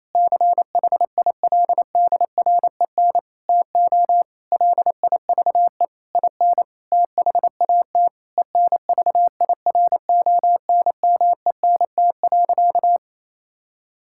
Morse code, heard as a keyed tone, about 28 wpm, 700 Hz. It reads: CHILDREN TO LIVE IN THAT ENVIRONMENT.
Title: Sentences from Top 1000 Words - Review Part 5 - 28wpm